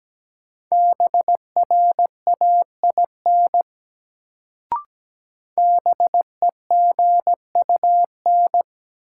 Morse code, heard as a keyed tone, about 17 wpm, 700 Hz. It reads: BRAIN BEGUN
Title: Sentences from Top 1000 Words - Review Part 6 - New Word Drill - 17wpm